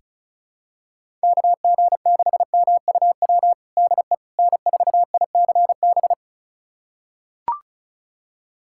Morse code, heard as a keyed tone, about 36 wpm, 700 Hz. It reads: KG6MUW DE N4ICB
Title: USA Callsign DE Callsign 35wpm